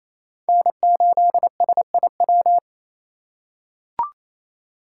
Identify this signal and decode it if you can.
Morse code, heard as a keyed tone, about 28 wpm, 700 Hz. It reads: N8SIW
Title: USA Call Signs - 5 Characters - 2x Character Spacing - 28wpm